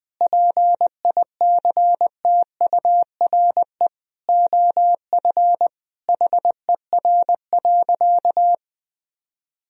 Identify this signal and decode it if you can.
Morse code, heard as a keyed tone, about 20 wpm, 700 Hz. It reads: PICTURE OF HER.
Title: Sentences from Top 1000 Words - Review Part 1 - 20wpm